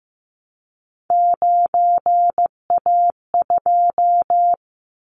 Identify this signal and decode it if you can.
Morse code, heard as a keyed tone, about 15 wpm, 700 Hz. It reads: 9A2
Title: Callsign Prefixes for Top 25 DXCC - Repeated 3x - 15wpm